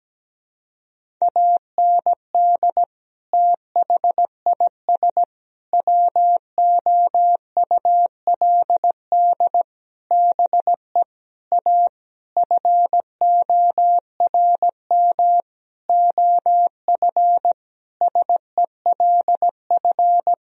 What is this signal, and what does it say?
Morse code, heard as a keyed tone, about 17 wpm, 700 Hz. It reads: AND THIS WOULD BE A FORM OF SELF
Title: Sentences from Top 1000 Words - Review Part 1 - 17wpm